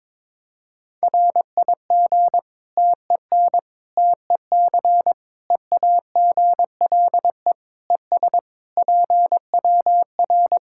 Morse code, heard as a keyed tone, about 22 wpm, 700 Hz. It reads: RIG TEN TEC EAGLE ES PWR
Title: Practice QSO - Exchange 2 - Info - 22wpm